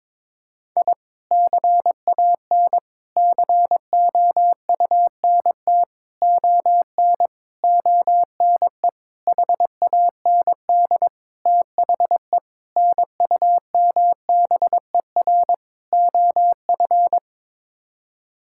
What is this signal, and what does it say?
Morse code, heard as a keyed tone, about 21 wpm, 700 Hz. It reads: I CAN COUNT ON ONE HAND THE NUMBER OF
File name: Sets of 9 Words - 22wpm